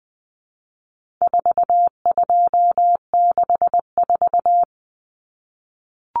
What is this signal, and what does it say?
Morse code, heard as a keyed tone, about 20 wpm, 700 Hz. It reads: 4264
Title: Numbers - 4-Digit 20wpm